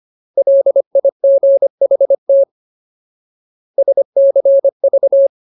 Morse code, heard as a keyed tone, about 25 wpm, 550 Hz. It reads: LIGHT SCV
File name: Ultimate Word List / 25WPM Ultimate